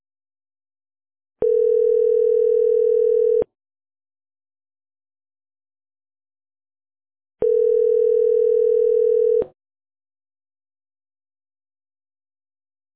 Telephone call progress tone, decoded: ringback tone